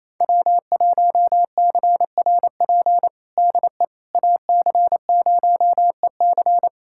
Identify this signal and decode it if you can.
Morse code, heard as a keyed tone, about 28 wpm, 700 Hz. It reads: W1CRP DE AC0EC
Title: USA Callsign DE Callsign - 5 Characters - 28wpm